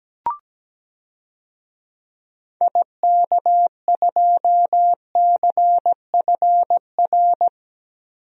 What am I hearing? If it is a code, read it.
Morse code, heard as a keyed tone, about 17 wpm, 700 Hz. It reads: IK2CFR
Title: DXCC-100 Call Signs - 6 Characters - Repeated 2x - 17wpm